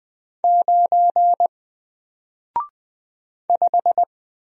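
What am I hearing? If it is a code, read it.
Morse code, heard as a keyed tone, about 20 wpm, 700 Hz. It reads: 9 5
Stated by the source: Single letter and number Morse drill at 20 wpm, from N4LCD.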